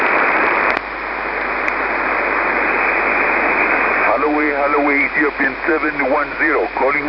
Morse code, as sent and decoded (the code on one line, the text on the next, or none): none